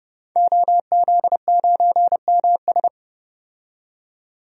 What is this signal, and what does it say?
Morse code, heard as a keyed tone, about 30 wpm, 700 Hz. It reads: OZ9MS
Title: DXCC-100 Call Signs - 5 Characters - 30wpm